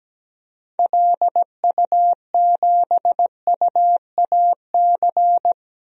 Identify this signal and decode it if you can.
Morse code, heard as a keyed tone, about 17 wpm, 700 Hz. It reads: LU7UAC
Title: DXCC-100 Call Signs - 6 Characters - Repeated 3x - 17wpm